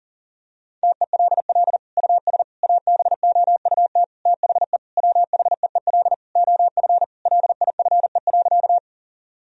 Morse code, heard as a keyed tone, about 40 wpm, 700 Hz. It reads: TELL US ABOUT THE WHEEL OF LIFE.
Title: Sentences from Top 1000 Words - Review Part 3 - 40wpm